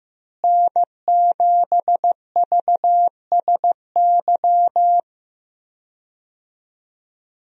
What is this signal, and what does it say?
Morse code, heard as a keyed tone, about 15 wpm, 700 Hz. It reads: N7VSY
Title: USA Call Signs - 5 Characters - Repeated 3x - 15wpm